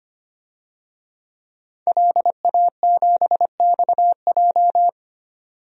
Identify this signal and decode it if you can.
Morse code, heard as a keyed tone, about 25 wpm, 700 Hz. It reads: LA7XJ
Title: DXCC-100 Call Signs - 5 Characters - 25wpm